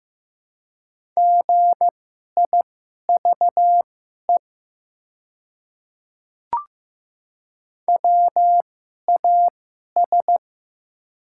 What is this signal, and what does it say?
Morse code, heard as a keyed tone, about 15 wpm, spaced as for 11 wpm, 700 Hz. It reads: GIVE WAS
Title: 2x Character Spacing - Top 200 Words - 15wpm